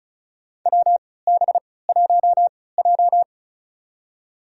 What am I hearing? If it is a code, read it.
Morse code, heard as a keyed tone, about 36 wpm, spaced as for 20 wpm, 700 Hz. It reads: WB1J
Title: USA Call Signs - 4 Characters - 3x Character Spacing - 35wpm